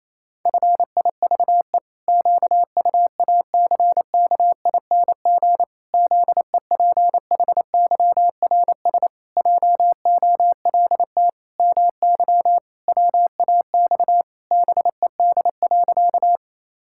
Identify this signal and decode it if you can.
Morse code, heard as a keyed tone, about 28 wpm, 700 Hz. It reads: FIVE QUACKING ZEPHYRS JOLT MY WAX BED.
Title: Pangram Challenge v2 - 28wpm